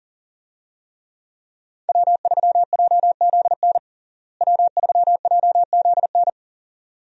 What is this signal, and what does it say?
Morse code, heard as a keyed tone, about 40 wpm, 700 Hz. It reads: W3JZN W3JZN
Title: USA Call Signs - 5 Characters - Repeated 2x - 40wpm